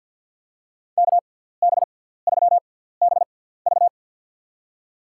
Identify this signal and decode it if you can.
Morse code, heard as a keyed tone, about 51 wpm, spaced as for 17 wpm, 700 Hz. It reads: KB3BV
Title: USA Call Signs - 6x Character Spacing - 50wpm